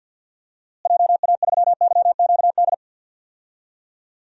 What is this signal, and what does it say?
Morse code, heard as a keyed tone, about 47 wpm, 700 Hz. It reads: JA3YQD
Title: DXCC-100 Call Signs - 6 Characters - Repeated 3x - 50wpm